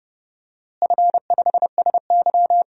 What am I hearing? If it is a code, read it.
Morse code, heard as a keyed tone, about 30 wpm, 700 Hz. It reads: F5SY
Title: DXCC-100 Call Signs - 4 Characters - Repeated 3x - 30wpm